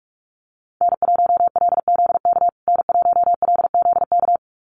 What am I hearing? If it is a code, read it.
Morse code, heard as a keyed tone, about 44 wpm, 700 Hz. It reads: N1LZK N1LZK
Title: USA Call Signs Repeated 2x 45wpm